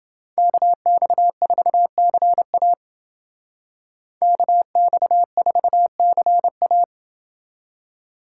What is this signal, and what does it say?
Morse code, heard as a keyed tone, about 29 wpm, 700 Hz. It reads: KX4CA KX4CA